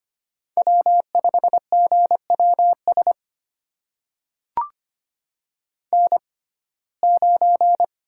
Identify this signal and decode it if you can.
Morse code, heard as a keyed tone, about 25 wpm, spaced as for 18 wpm, 700 Hz. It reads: W5GWS N9
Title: USA Call Signs - 5 Characters - 6x Character Spacing - 25wpm